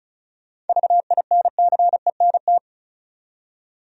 Morse code, heard as a keyed tone, about 36 wpm, 700 Hz. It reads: VINCENT
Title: USA Common Names - 35wpm